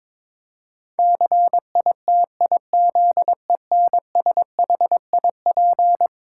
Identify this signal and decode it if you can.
Morse code, heard as a keyed tone, about 22 wpm, 700 Hz. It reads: CITIZENSHIP
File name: practice / 22wpm / Eleven Letter Words - 22wpm